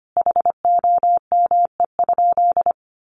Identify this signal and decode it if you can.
Morse code, heard as a keyed tone, about 25 wpm, 700 Hz. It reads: HOME?
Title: Sentences from Top 1100 Words 25wpm